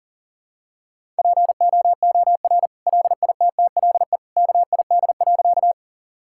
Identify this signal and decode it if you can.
Morse code, heard as a keyed tone, about 37 wpm, 700 Hz. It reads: POOR LITTLE KID.
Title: Sentences from Top 1600 Words 40wpm